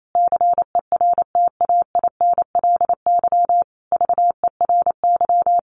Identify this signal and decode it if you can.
Morse code, heard as a keyed tone, about 28 wpm, 700 Hz. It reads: CERTAINLY VERY
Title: Sentences from Top 1200 Words 28wpm